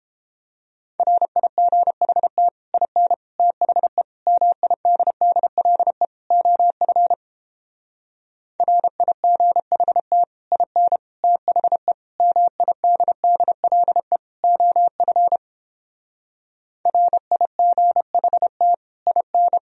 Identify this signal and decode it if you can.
Morse code, heard as a keyed tone, about 29 wpm, 700 Hz. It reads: RIGHT IN THE MIDDLE OF RIGHT IN THE MIDDLE OF RIGHT IN